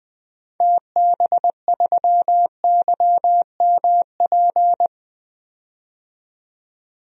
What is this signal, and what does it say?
Morse code, heard as a keyed tone, about 20 wpm, 700 Hz. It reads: TB3YMP
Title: DXCC-100 Call Signs - 6 Characters - 20wpm